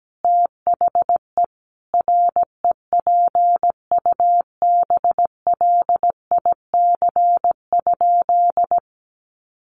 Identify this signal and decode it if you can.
Morse code, heard as a keyed tone, about 17 wpm, 700 Hz. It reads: THE REPUBLIC?
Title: Sentences from Top 1500 Words 17wpm